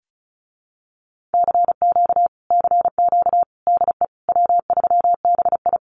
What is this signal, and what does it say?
Morse code, heard as a keyed tone, about 34 wpm, 700 Hz. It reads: CQ CQ DE W3BI
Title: Common QSO Phrases Encore 35wpm